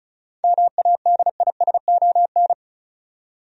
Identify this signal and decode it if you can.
Morse code, heard as a keyed tone, about 36 wpm, 700 Hz. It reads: MADISON
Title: USA Common Names - 35wpm